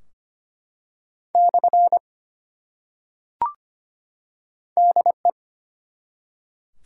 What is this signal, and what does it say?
Morse code, heard as a keyed tone, about 25 wpm, 700 Hz. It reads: / DE
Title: Top 180 QSO Elements 25wpm